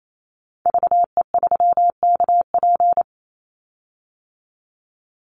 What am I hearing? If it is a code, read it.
Morse code, heard as a keyed tone, about 28 wpm, 700 Hz. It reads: VE3KP
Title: Call Signs - 28wpm